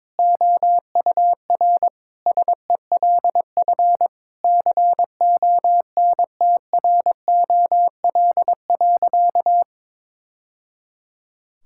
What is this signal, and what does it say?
Morse code, heard as a keyed tone, about 22 wpm, 700 Hz. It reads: OUR SELF CONTROL.
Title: Sentences from Top 1000 Words - Review Part 5 - 22wpm